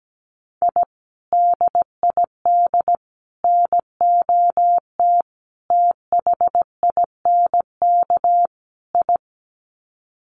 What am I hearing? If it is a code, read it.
Morse code, heard as a keyed tone, about 17 wpm, 700 Hz. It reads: I DID NOT THINK I
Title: Sets of 5 Words Encore 17wpm